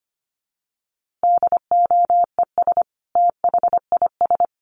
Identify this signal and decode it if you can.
Morse code, heard as a keyed tone, about 25 wpm, 700 Hz. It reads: DOES THIS
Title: A Sound Of Thunder 25wpm